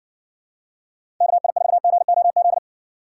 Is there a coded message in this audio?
Yes: Morse code from 1.2 to 2.6 s, heard as a keyed tone, about 60 wpm, 700 Hz. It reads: KI4GOZ